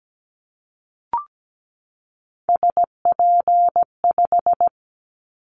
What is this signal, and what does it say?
Morse code, heard as a keyed tone, about 17 wpm, 700 Hz. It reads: SP5
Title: Callsign Prefixes for Top 25 DXCC - 17wpm